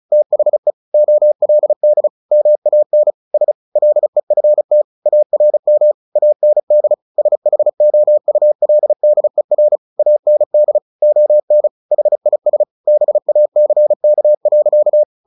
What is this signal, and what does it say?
Morse code, heard as a keyed tone, about 34 wpm, 600 Hz. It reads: THE OLD MAN S LEFT ARM AND SHOULDER AND ON HIS BACK.
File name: the old man and the sea / the old man and the sea 1 35wpm